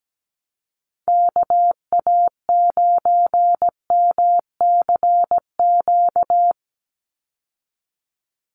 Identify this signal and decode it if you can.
Morse code, heard as a keyed tone, about 17 wpm, 700 Hz. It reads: KA9MCQ
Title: USA Call Signs 17wpm